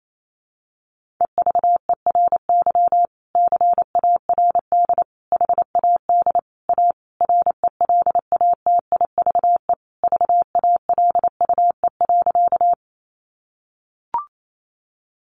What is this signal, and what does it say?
Morse code, heard as a keyed tone, about 28 wpm, 700 Hz. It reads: EVERY CARD HAD A RELATIVE VALUE.
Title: Sentences from Top 1400 Words 28wpm